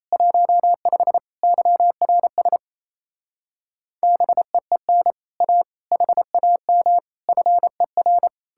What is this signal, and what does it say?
Morse code, heard as a keyed tone, about 37 wpm, 700 Hz. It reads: 15 YRS BEEN A HAM FER